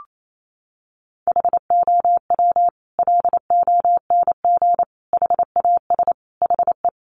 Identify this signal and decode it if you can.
Morse code, heard as a keyed tone, about 28 wpm, 700 Hz. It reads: HOW LONG HAS HE
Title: Sentences from Top 100 Words 28wpm